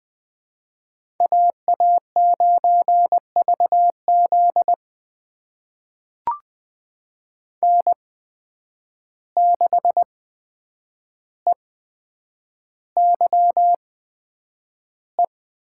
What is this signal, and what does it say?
Morse code, heard as a keyed tone, about 20 wpm, spaced as for 9 wpm, 700 Hz. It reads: AA9VZ N6EYE
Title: USA Call Signs - 5 Characters - 8x Character Spacing - 20wpm